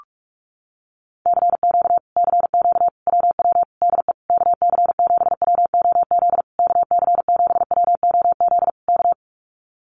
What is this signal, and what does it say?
Morse code, heard as a keyed tone, about 46 wpm, 700 Hz. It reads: CQ CQ WW DE KC7WOZ KC7WOZ K
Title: Common QSO Phrases Encore 45wpm